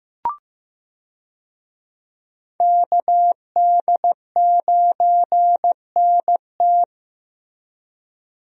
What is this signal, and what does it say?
Morse code, heard as a keyed tone, about 15 wpm, 700 Hz. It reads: KD9NT
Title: USA Call Signs - 5 Characters - Repeated 2x - 15wpm